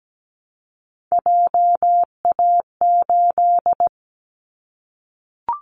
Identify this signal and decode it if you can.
Morse code, heard as a keyed tone, about 17 wpm, 700 Hz. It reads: JA8
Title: Callsign Prefixes for Top 50 DXCC - 17wpm